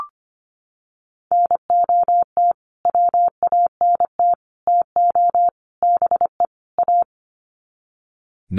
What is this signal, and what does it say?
Morse code, heard as a keyed tone, about 25 wpm, 700 Hz. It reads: NOT WANT TO BE A
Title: Sets of 5 Words Encore 25wpm